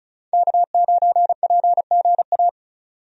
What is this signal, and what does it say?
Morse code, heard as a keyed tone, about 34 wpm, 700 Hz. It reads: K9PGA
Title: USA Call Signs - 5 Characters - 35wpm